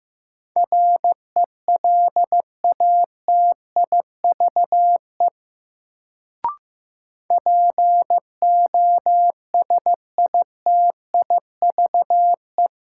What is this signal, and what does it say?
Morse code, heard as a keyed tone, about 15 wpm, 700 Hz. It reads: RELATIVE POSITIVE
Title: Sentences from Top 1400 Words - New Word Drill Alt - 15wpm